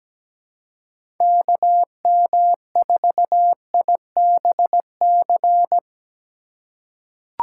Morse code, heard as a keyed tone, about 17 wpm, 700 Hz. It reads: KM4IBC
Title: USA Call Signs - 4x Character Spacing - 17wpm